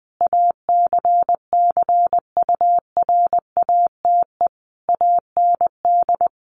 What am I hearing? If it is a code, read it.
Morse code, heard as a keyed tone, about 20 wpm, 700 Hz. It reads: ACCURATE AND
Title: Sentences from Top 1200 Words 20wpm